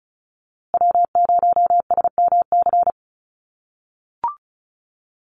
Morse code, heard as a keyed tone, about 34 wpm, 700 Hz. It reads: W0SMC
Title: USA Call Signs Repeated 2x 35wpm